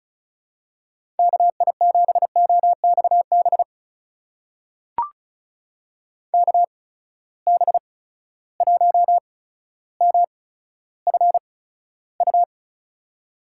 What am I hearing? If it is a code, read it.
Morse code, heard as a keyed tone, about 36 wpm, spaced as for 15 wpm, 700 Hz. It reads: KI7OXB KB1MFU
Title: USA Call Signs - 6 Characters - 8x Character Spacing - 35wpm